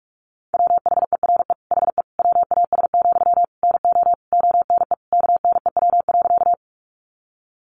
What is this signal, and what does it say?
Morse code, heard as a keyed tone, about 44 wpm, 700 Hz. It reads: WHERE HE WAS, NO ONE KNEW.
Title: Sentences from Top 400 Words 45wpm